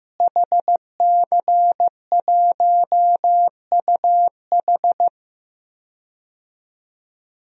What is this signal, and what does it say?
Morse code, heard as a keyed tone, about 15 wpm, 700 Hz. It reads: HC1UH
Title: DXCC-100 Call Signs - 5 Characters - Repeated 3x - 15wpm